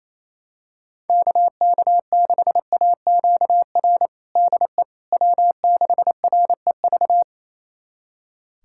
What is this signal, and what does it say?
Morse code, heard as a keyed tone, about 28 wpm, 700 Hz. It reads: KK6AQR DE W6REV
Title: USA Callsign DE Callsign 28wpm